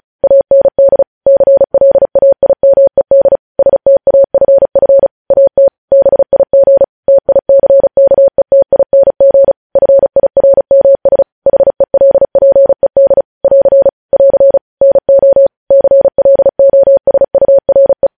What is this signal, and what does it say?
Morse code, heard as a keyed tone, about 36 wpm, 550 Hz. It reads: AND CLAIMED STAFF AT BIG TICKETING FIRMS HELPED <AR> <AR> NO CLOSURE
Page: news headlines in morse code at 35 wpm